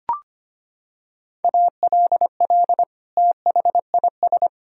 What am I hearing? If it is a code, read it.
Morse code, heard as a keyed tone, about 25 wpm, 700 Hz. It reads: ALL THIS
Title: Sentences from Top 1000 Words - Review Part 4 - 25wpm